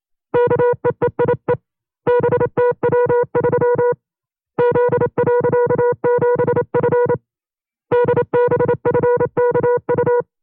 Morse code, heard as a keyed tone, about 27 wpm, 480 Hz. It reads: KEEIE BTW3 Z.7F DBFKU
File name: FreeFiles / CW-practice / 25WPM-side1